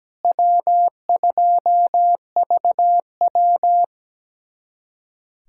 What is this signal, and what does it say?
Morse code, heard as a keyed tone, about 17 wpm, 700 Hz. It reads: W2VW
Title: USA Callsign DE Callsign - 4 Characters - 17wpm